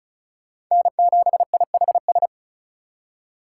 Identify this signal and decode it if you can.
Morse code, heard as a keyed tone, about 47 wpm, spaced as for 41 wpm, 700 Hz. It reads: N7IHS
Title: Speed Racing - USA 1x3 Call Signs 35wpm